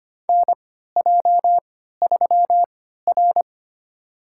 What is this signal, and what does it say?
Morse code, heard as a keyed tone, about 25 wpm, spaced as for 14 wpm, 700 Hz. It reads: NJ3R
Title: USA Call Signs - 3x Character Spacing - 25wpm